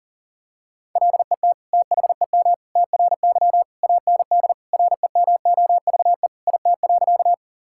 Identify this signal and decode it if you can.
Morse code, heard as a keyed tone, about 40 wpm, 700 Hz. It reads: LET THEM TRY AND REMOVE IT.
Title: Sentences from Top 1000 Words - Review Part 5 - 40wpm